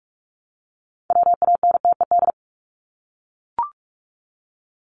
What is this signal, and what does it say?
Morse code, heard as a keyed tone, about 44 wpm, 700 Hz. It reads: WANTED
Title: ED Suffix - Top 100 Words with Base Words 45wpm